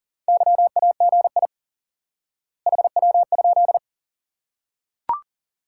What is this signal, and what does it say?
Morse code, heard as a keyed tone, about 37 wpm, 700 Hz. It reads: YAGI HW?
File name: course / instant-qso-element-course-3 / IQE Course 3 - Lesson 89 - YAGI - Two Elements - 40wpm